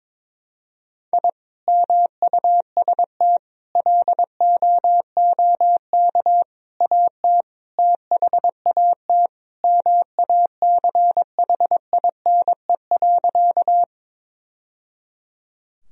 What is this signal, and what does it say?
Morse code, heard as a keyed tone, about 22 wpm, 700 Hz. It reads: I MUST LOOK AT THAT MACHINE.
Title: Sentences from Top 1000 Words - Review Part 2 - 22wpm